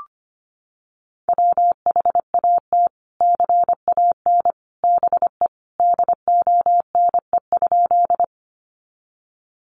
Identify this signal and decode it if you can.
Morse code, heard as a keyed tone, about 25 wpm, 700 Hz. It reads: WHAT CAN BE DONE?
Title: Sentences from Top 500 Words 25wpm